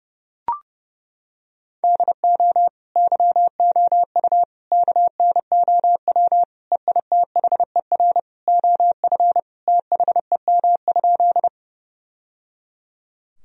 Morse code, heard as a keyed tone, about 30 wpm, 700 Hz. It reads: DO YOU KNOW EITHER OF THEM?
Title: Sentences from Top 1000 Words - Review Part 4 - 30wpm